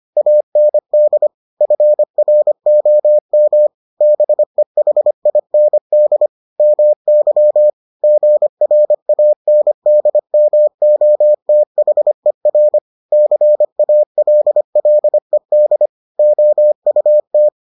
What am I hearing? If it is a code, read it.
Morse code, heard as a keyed tone, about 25 wpm, 600 Hz. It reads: AND FROM BEHIND MY GRANDMOTHER CALLED OUT